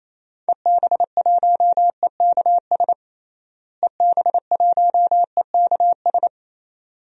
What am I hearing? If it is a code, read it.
Morse code, heard as a keyed tone, about 27 wpm, 700 Hz. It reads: EB1EKS EB1EKS
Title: DXCC-100 Call Signs - 6 Characters - Repeated 2x - 28wpm